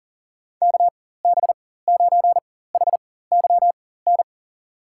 Morse code, heard as a keyed tone, about 40 wpm, spaced as for 19 wpm, 700 Hz. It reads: KB9HYN
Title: USA Call Signs - 4x Character Spacing - 40wpm